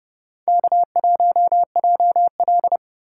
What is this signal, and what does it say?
Morse code, heard as a keyed tone, about 35 wpm, 700 Hz. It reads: K1JL
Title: Speed Racing - USA 1x2 Call Signs 30wpm